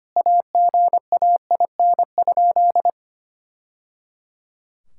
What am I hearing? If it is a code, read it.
Morse code, heard as a keyed tone, about 25 wpm, 700 Hz. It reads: AGAIN?
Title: Sentences from Top 1000 Words - Review Part 5 - 25wpm